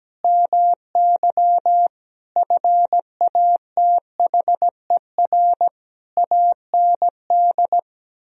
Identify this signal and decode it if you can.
Morse code, heard as a keyed tone, about 17 wpm, 700 Hz. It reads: MY FATHER AND
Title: Sentences from Top 1000 Words - Review Part 3 - 17wpm